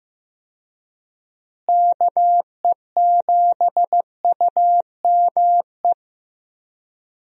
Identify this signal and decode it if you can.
Morse code, heard as a keyed tone, about 15 wpm, 700 Hz. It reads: KE7UME
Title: USA Call Signs - 6 Characters - Repeated 2x - 15wpm